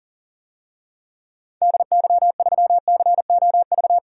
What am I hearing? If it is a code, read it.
Morse code, heard as a keyed tone, about 40 wpm, 700 Hz. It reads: DY3COV
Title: DXCC-100 Call Signs - 6 Characters - 40wpm